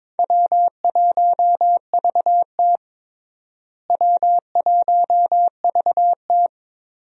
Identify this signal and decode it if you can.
Morse code, heard as a keyed tone, about 21 wpm, 700 Hz. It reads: W1VT W1VT
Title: USA Call Signs - 4 Characters - Repeated 3x - 22wpm